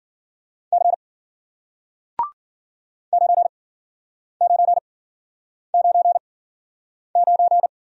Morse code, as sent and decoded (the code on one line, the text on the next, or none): -.-. / ----. ----. ----. ----.
C 9999